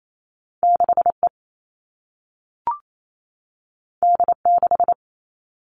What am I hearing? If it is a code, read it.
Morse code, heard as a keyed tone, about 28 wpm, 700 Hz. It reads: 6E D6
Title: All Cut Numbers - 2-Digit 28wpm